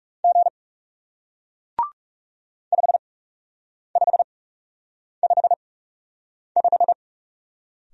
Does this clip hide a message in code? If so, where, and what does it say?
Yes: Morse code from 0.2 to 6.9 s, heard as a keyed tone, about 40 wpm, spaced as for 8 wpm, 700 Hz. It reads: G 5555